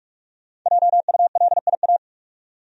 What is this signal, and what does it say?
Morse code, heard as a keyed tone, about 46 wpm, 700 Hz. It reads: JULIA
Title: USA Common Names - 45wpm